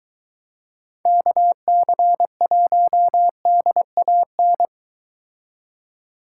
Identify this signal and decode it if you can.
Morse code, heard as a keyed tone, about 23 wpm, 700 Hz. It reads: KC1DAN